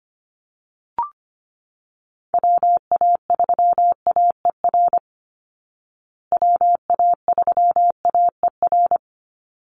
Morse code, heard as a keyed tone, about 25 wpm, 700 Hz. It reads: WA3AER WA3AER
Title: Call Signs Repeated 3x - 25wpm